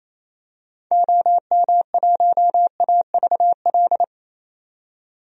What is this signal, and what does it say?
Morse code, heard as a keyed tone, about 28 wpm, 700 Hz. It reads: OM1AVL
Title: DXCC-100 Call Signs - 6 Characters - 28wpm